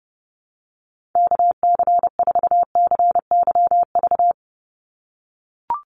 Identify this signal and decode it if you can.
Morse code, heard as a keyed tone, about 30 wpm, 700 Hz. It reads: KC4CYV
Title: USA Call Signs 30wpm